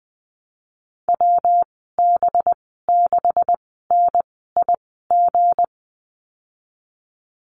Morse code, heard as a keyed tone, about 20 wpm, spaced as for 15 wpm, 700 Hz. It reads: WB6NIG
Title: 186 Lesson 30g - Letter b - Call Signs 20wpm